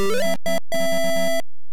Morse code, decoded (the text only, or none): none